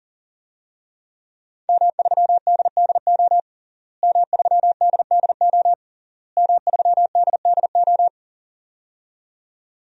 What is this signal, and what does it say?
Morse code, heard as a keyed tone, about 40 wpm, 700 Hz. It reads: M3DDO M3DDO M3DDO
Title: DXCC-100 Call Signs - 5 Characters - Repeated 3x - 40wpm